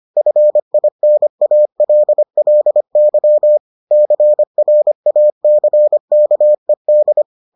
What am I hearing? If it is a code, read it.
Morse code, heard as a keyed tone, about 25 wpm, 600 Hz. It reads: FINALLY CRACKED